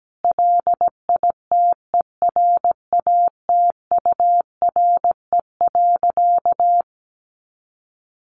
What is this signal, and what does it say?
Morse code, heard as a keyed tone, about 17 wpm, 700 Hz. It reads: LITERATURE.
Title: Sentences from Top 900 Words 17wpm